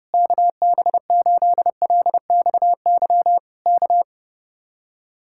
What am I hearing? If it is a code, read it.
Morse code, heard as a keyed tone, about 29 wpm, 700 Hz. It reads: KB8LXY K
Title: Speed-Racing Common QSO Phrases - 30wpm